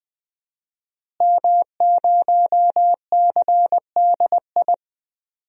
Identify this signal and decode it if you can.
Morse code, heard as a keyed tone, about 20 wpm, 700 Hz. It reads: M0CDI
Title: DXCC-100 Call Signs - 5 Characters - Repeated 3x - 20wpm